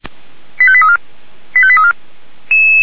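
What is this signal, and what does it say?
Five-tone selective call (ZVEI), heard as quick runs of five tones, tones of about 70 ms. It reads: 85834 85834
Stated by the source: ZVEI five-tone selective call received on an SDR.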